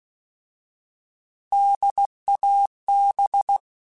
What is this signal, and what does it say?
Morse code, heard as a keyed tone, about 16 wpm, 800 Hz. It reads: DAB
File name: Alphabetical / Alphabetical mp3s / Ds @15